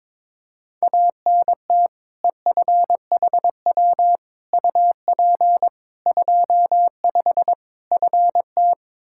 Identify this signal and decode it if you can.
Morse code, heard as a keyed tone, about 22 wpm, 700 Hz. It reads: ANT EFHW UP 25 FT
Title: K1USN SST QSO Practice 22wpm